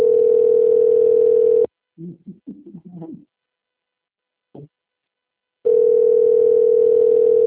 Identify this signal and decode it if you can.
Ringback tone (North America), 440 + 480 Hz.
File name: out- - - - - .58902